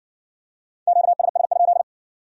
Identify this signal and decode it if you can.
Morse code, heard as a keyed tone, about 56 wpm, 700 Hz. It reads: QSS?